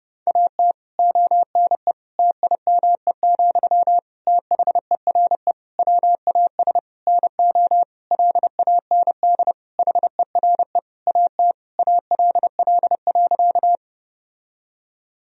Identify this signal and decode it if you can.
Morse code, heard as a keyed tone, about 32 wpm, 700 Hz. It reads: AT ONE TIME, THERE WAS NO LAND HERE AT ALL.
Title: Sentences from Top 1000 Words - Review Part 1 - 30wpm